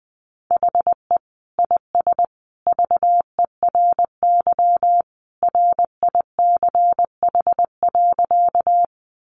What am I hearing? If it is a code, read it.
Morse code, heard as a keyed tone, about 20 wpm, 700 Hz. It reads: HE IS VERY RICH.
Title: Sentences from Top 1100 Words 20wpm